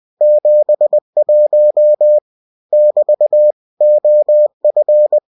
Morse code, heard as a keyed tone, about 20 wpm, 600 Hz. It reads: 71 <BT> OF
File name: cw-books / winnie-the-pooh / winnie-20wpm-page-0071